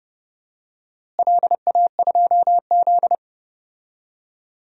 Morse code, heard as a keyed tone, about 31 wpm, 700 Hz. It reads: LA2Z
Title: DXCC-100 Call Signs - 4 Characters - Repeated 3x - 30wpm